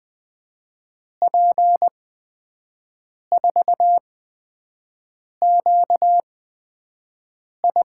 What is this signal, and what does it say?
Morse code, heard as a keyed tone, about 20 wpm, spaced as for 5 wpm, 700 Hz. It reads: P4QI
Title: USA Call Signs - 5 Characters - 8x Character Spacing - 20wpm